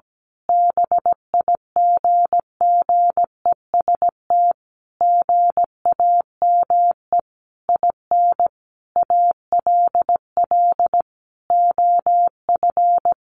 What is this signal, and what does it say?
Morse code, heard as a keyed tone, about 17 wpm, 700 Hz. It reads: BIGGEST GAME IN ALL OF
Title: A Sound Of Thunder 17wpm